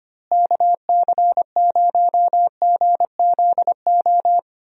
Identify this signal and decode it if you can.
Morse code, heard as a keyed tone, about 25 wpm, 700 Hz. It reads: KC0GZO
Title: USA Call Signs - 6 Characters - Repeated 2x - 25wpm